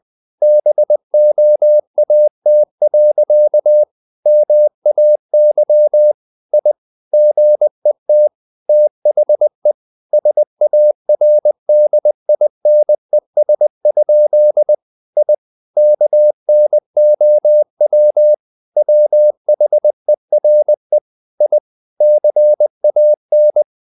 Morse code, heard as a keyed tone, about 20 wpm, 600 Hz. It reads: BOAT. MAY I GET THE SARDINES? I KNOW WHERE I CAN
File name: the old man and the sea / the old man and the sea part 1 20wpm